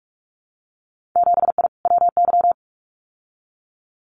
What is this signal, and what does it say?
Morse code, heard as a keyed tone, about 46 wpm, 700 Hz. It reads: 7I WY
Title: Winter Field Day Exchanges Encore 45wpm